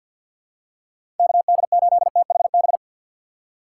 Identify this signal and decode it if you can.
Morse code, heard as a keyed tone, about 53 wpm, 700 Hz. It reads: KD8THB
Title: USA Call Signs - 6x Character Spacing - 50wpm